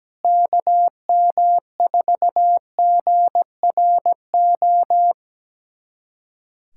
Morse code, heard as a keyed tone, about 17 wpm, 700 Hz. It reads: KM4GRO
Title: USA Callsign DE Callsign 17wpm